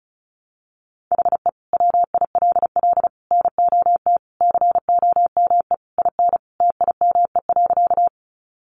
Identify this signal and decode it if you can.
Morse code, heard as a keyed tone, about 34 wpm, 700 Hz. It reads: HE WILL NOT COME IN TIME.
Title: Sentences from Top 600 Words 35wpm